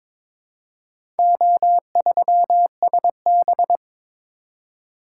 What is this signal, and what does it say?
Morse code, heard as a keyed tone, about 21 wpm, 700 Hz. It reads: O3SB
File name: practice / 22wpm / Four Letter-Number Combinations - 22wpm